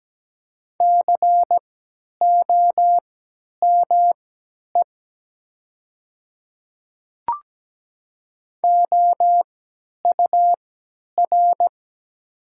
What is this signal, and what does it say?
Morse code, heard as a keyed tone, about 17 wpm, spaced as for 10 wpm, 700 Hz. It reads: COME OUR
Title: 3x Character Spacing - Top 500 Words - 17wpm